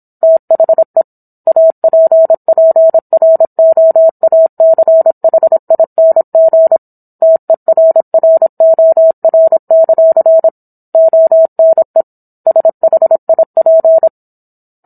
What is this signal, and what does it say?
Morse code, heard as a keyed tone, about 27 wpm, 650 Hz. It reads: THE APPROACHING TERROR; ONE SHIP